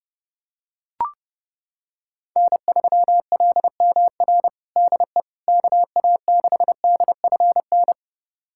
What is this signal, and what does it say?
Morse code, heard as a keyed tone, about 29 wpm, 700 Hz. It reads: N3LMR DE KA6DFN
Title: USA Callsign DE Callsign 30wpm